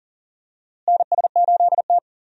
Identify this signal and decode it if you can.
Morse code, heard as a keyed tone, about 40 wpm, 700 Hz. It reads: NS8T
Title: USA Call Signs - 4 Characters - Repeated 2x - 40wpm